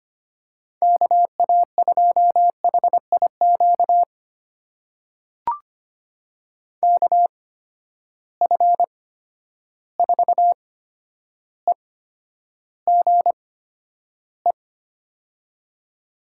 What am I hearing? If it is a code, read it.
Morse code, heard as a keyed tone, about 25 wpm, spaced as for 11 wpm, 700 Hz. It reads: KA2HIQ KF4EGE
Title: USA Call Signs - 8x Character Spacing - 25wpm